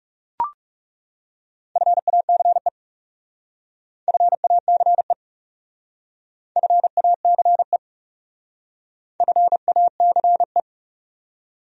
Morse code, heard as a keyed tone, about 36 wpm, 700 Hz. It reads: FACE FACE FACE FACE
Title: Speed-Racing Top 500 Words 30wpm